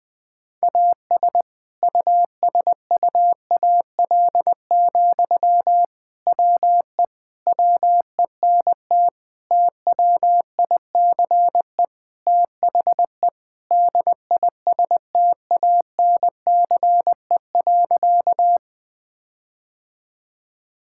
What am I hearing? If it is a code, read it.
Morse code, heard as a keyed tone, about 21 wpm, 700 Hz. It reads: AS USUAL, WE WENT TWICE THE DISTANCE.
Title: Sentences from Top 1600 Words 20wpm